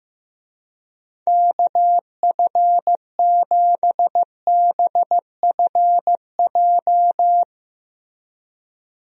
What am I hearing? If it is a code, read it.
Morse code, heard as a keyed tone, about 15 wpm, 700 Hz. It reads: KF7BFJ
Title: USA Call Signs - 6 Characters - Repeated 2x - 15wpm